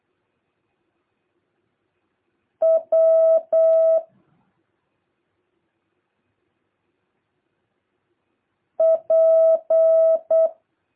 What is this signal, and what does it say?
Morse code, heard as a keyed tone, about 8 wpm, 650 Hz. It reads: W P
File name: jh8cbh / 2023cw